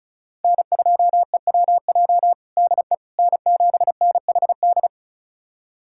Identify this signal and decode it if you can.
Morse code, heard as a keyed tone, about 36 wpm, 700 Hz. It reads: N2EWJ DE N7NHD
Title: USA Callsign DE Callsign - 5 Characters - 35wpm